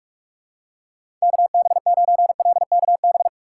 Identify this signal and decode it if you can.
Morse code, heard as a keyed tone, about 46 wpm, 700 Hz. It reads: KB9LKB
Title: USA Call Signs - 6x Character Spacing - 45wpm